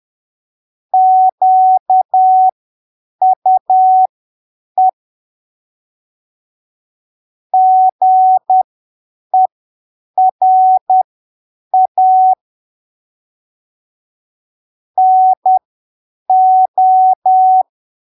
Morse code, heard as a keyed tone, about 10 wpm, spaced as for 7 wpm, 750 Hz. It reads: QUE GERA NO
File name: arquivos / texto / texto 27 10ppm